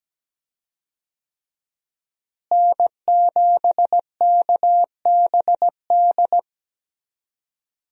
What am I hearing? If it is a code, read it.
Morse code, heard as a keyed tone, about 17 wpm, 700 Hz. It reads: N7KBD
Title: USA Call Signs - 5 Characters - Repeated 3x - 17wpm